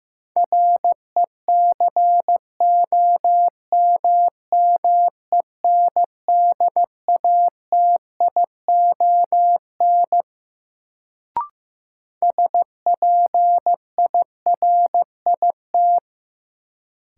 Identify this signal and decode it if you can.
Morse code, heard as a keyed tone, about 15 wpm, 700 Hz. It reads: RECOMMENDATION SPIRIT
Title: Sentences from Top 1400 Words - New Word Drill - 15wpm